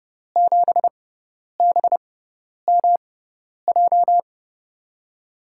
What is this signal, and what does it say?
Morse code, heard as a keyed tone, about 32 wpm, spaced as for 10 wpm, 700 Hz. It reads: 7BMJ
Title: USA Call Signs - 5 Characters - 6x Character Spacing - 30wpm